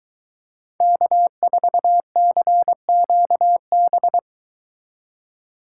Morse code, heard as a keyed tone, about 23 wpm, 700 Hz. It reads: K4CQB